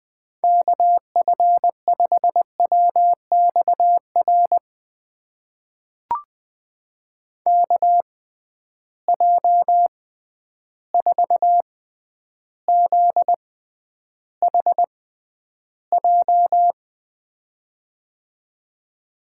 Morse code, heard as a keyed tone, about 20 wpm, spaced as for 10 wpm, 700 Hz. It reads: KF5WXR KJ4ZHJ
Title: USA Call Signs - 6 Characters - 6x Character Spacing - 20wpm